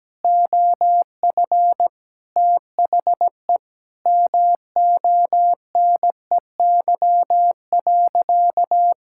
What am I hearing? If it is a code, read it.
Morse code, heard as a keyed tone, about 17 wpm, 700 Hz. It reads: OF THE MONEY.
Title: Sentences from Top 1000 Words - Review Part 6 - 17wpm